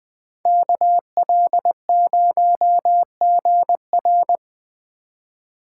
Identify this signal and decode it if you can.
Morse code, heard as a keyed tone, about 20 wpm, 700 Hz. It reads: KL0GR